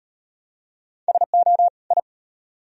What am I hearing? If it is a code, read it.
Morse code, heard as a keyed tone, about 36 wpm, 700 Hz. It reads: SO I